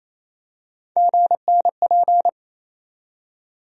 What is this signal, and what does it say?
Morse code, heard as a keyed tone, about 28 wpm, 700 Hz. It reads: GNP
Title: USA Call Sign Postfixes - 3 Letters 28wpm